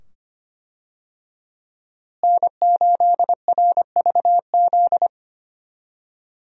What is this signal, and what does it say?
Morse code, heard as a keyed tone, about 25 wpm, 700 Hz. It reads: N8RVZ
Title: USA Call Signs - 5 Characters - Repeated 2x - 25wpm